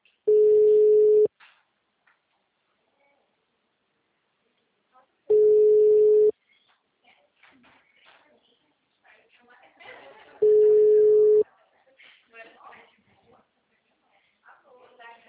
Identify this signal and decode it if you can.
Ringback tone (Europe, 425 Hz), with speech.